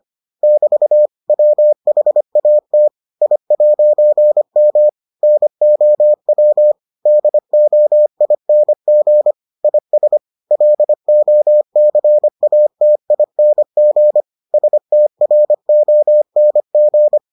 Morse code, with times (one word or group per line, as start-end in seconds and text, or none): 0.4-1.1 <BT>
1.3-2.9 WHAT
3.2-4.9 I'M
5.2-6.7 NOW
7.1-9.3 DOING
9.6-10.2 IS
10.5-14.2 LOCATING
14.5-17.2 STRONG